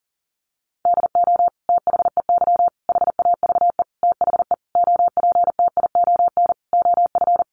Morse code, heard as a keyed tone, about 40 wpm, 700 Hz. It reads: DO THEY HAVE THE OPTION OF
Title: Sentences from Top 1300 Words 40wpm